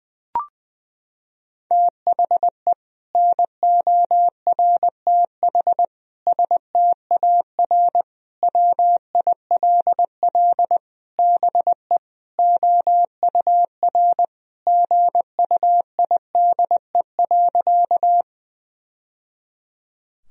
Morse code, heard as a keyed tone, about 20 wpm, 700 Hz. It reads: THE NORTH STAR WILL BE OUR GUIDE.
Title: Sentences from Top 1000 Words - Review Part 3 - 20wpm